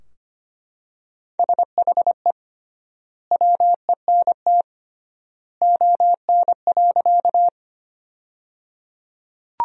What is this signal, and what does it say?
Morse code, heard as a keyed tone, about 25 wpm, 700 Hz. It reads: SHE WENT ON.
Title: 3x Word-Spacing - Sentences from Top 200 Words - 25wpm